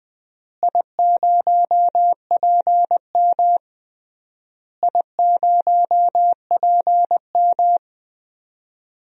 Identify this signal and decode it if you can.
Morse code, heard as a keyed tone, about 21 wpm, 700 Hz. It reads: I0PM I0PM
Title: DXCC-100 Call Signs - 4 Characters - Repeated 3x - 20wpm